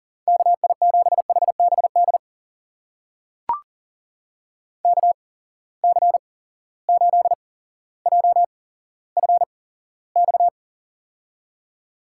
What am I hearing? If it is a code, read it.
Morse code, heard as a keyed tone, about 39 wpm, spaced as for 17 wpm, 700 Hz. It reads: KI7HBD KC8JFX
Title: USA Call Signs - 6 Characters - 8x Character Spacing - 40wpm